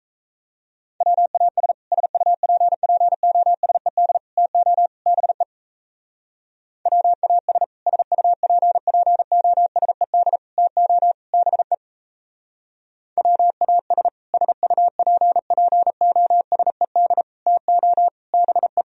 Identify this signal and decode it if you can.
Morse code, heard as a keyed tone, about 35 wpm, 700 Hz. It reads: WAS SUPPOSED TO BE WAS SUPPOSED TO BE WAS SUPPOSED TO BE
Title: Speed-Racing - Sets of 4 Words 28wpm